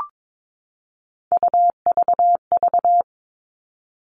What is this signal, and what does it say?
Morse code, heard as a keyed tone, about 22 wpm, 700 Hz. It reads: UVV
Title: Three Challenging Letter-Number Combinations 22wpm